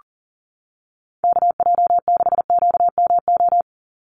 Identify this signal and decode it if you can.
Morse code, heard as a keyed tone, about 40 wpm, 700 Hz. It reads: KJ6QMO
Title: USA Call Signs 40wpm